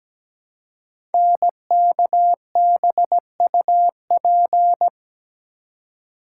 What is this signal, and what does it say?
Morse code, heard as a keyed tone, about 17 wpm, 700 Hz. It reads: NKBUP
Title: Code Groups - 5 Letters - 17wpm